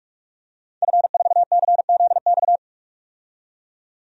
Morse code, heard as a keyed tone, about 48 wpm, 700 Hz. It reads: F4CZX
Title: DXCC-100 Call Signs - 5 Characters - 45wpm